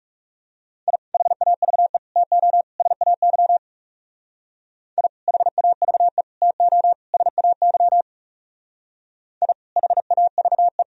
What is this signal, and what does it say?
Morse code, heard as a keyed tone, about 34 wpm, 700 Hz. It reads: I HAVE TO SAY I HAVE TO SAY I HAVE
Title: Speed-Racing - Sets of 4 Words 30wpm